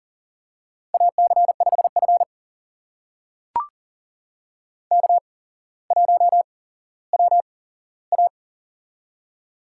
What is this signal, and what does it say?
Morse code, heard as a keyed tone, about 40 wpm, spaced as for 17 wpm, 700 Hz. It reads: AC5F K1WA
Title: USA Call Signs - 4 Characters - 8x Character Spacing - 40wpm